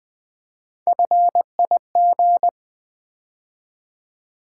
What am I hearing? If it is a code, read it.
Morse code, heard as a keyed tone, about 20 wpm, 700 Hz. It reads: FIG
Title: Build A Word - 3 Letters At A Time - 6-Letter Words - 20wpm